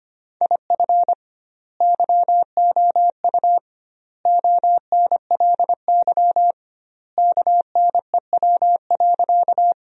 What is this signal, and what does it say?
Morse code, heard as a keyed tone, about 25 wpm, 700 Hz. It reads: IF YOU ONLY KNEW.
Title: 2x Word-Spacing - Sentences from Top 400 Words - 25wpm